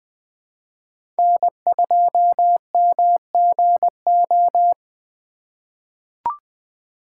Morse code, heard as a keyed tone, about 20 wpm, 700 Hz. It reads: N2MGO
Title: USA Call Signs - 4x Character Spacing - 20wpm